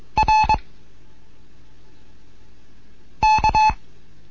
Morse code, heard as a keyed tone, about 22 wpm, 860 Hz. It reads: R K